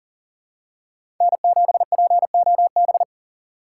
Morse code, heard as a keyed tone, about 40 wpm, 700 Hz. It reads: N7POB
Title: USA Call Signs - 4x Character Spacing - 40wpm